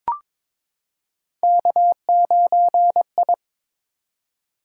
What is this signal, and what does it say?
Morse code, heard as a keyed tone, about 22 wpm, 700 Hz. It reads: K9I
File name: practice / 22wpm / Three Letter-Number Combinations - 22wpm